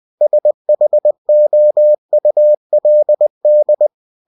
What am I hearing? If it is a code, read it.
Morse code, heard as a keyed tone, about 20 wpm, 600 Hz. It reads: SHOULD
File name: cw-books / winnie-the-pooh / winnie-20wpm-page-0036